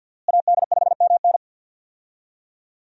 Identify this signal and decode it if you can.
Morse code, heard as a keyed tone, about 51 wpm, 700 Hz. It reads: AD5MN
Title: USA Call Signs - 5 Characters - Repeated 3x - 50wpm